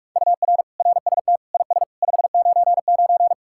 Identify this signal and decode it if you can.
Morse code, heard as a keyed tone, about 66 wpm, spaced as for 55 wpm, 700 Hz. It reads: UR RST IS 599
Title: Speed-Racing Common QSO Phrases - 45wpm